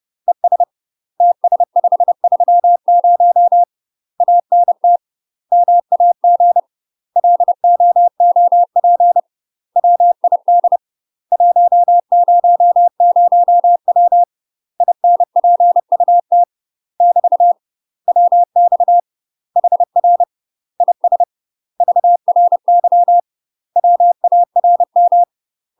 Morse code, heard as a keyed tone, about 30 wpm, 700 Hz. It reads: ES TS530 ANT MAG LOOP WID 100W INPUT <BT> WX HR IS VRY WARM
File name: UK MASTER QSOs TRAINING / 30WPM / UK MASTER QSOs 0015